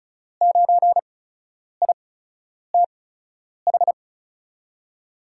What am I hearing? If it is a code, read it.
Morse code, heard as a keyed tone, about 36 wpm, spaced as for 10 wpm, 700 Hz. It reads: 9ITH
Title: USA Call Signs - 6 Characters - 8x Character Spacing - 35wpm